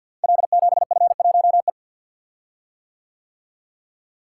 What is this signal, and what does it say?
Morse code, heard as a keyed tone, about 50 wpm, 700 Hz. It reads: L7F1E
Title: Code Groups - 5 Letter:Numbers - 50wpm